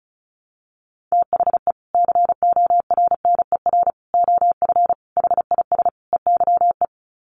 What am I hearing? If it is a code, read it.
Morse code, heard as a keyed tone, about 34 wpm, 700 Hz. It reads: THE CORNER OF HIS EYE
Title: Sets of 5 Words Encore2 35wpm